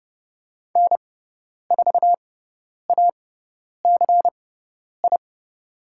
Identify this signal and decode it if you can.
Morse code, heard as a keyed tone, about 30 wpm, spaced as for 10 wpm, 700 Hz. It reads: N4ACI
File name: Lesson 16g - Letter c - Call Signs 30wpm